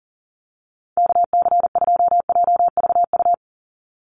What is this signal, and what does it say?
Morse code, heard as a keyed tone, about 40 wpm, 700 Hz. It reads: KC2JVU
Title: USA Call Signs 40wpm